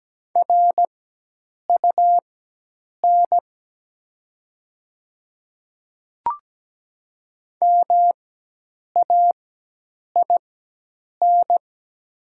Morse code, heard as a keyed tone, about 17 wpm, spaced as for 8 wpm, 700 Hz. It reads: RUN MAIN
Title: 4x Character Spacing - Top 300 Words - 17wpm